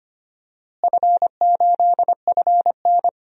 Morse code, heard as a keyed tone, about 25 wpm, 700 Hz. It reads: F8FN
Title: DXCC-100 Call Signs - 4 Characters - Repeated 2x - 25wpm